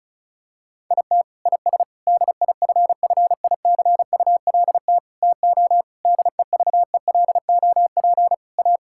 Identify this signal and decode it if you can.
Morse code, heard as a keyed tone, about 36 wpm, 700 Hz. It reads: IT IS DIFFICULT TO DEVELOP A
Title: Sentences from Top 1000 Words - Review Part 5 - 35wpm